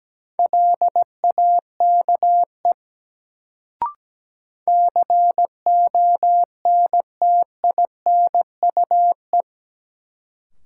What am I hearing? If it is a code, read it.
Morse code, heard as a keyed tone, about 17 wpm, 700 Hz. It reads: LAKE CONTINUE
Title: Sentences from Top 1100 Words - New Word Drill Alt - 17wpm